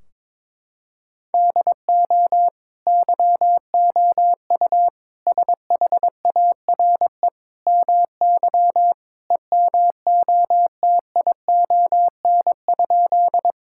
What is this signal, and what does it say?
Morse code, heard as a keyed tone, about 22 wpm, 700 Hz. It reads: DO YOU SHARE MY EMOTION?
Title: Sentences from Top 1600 Words 22wpm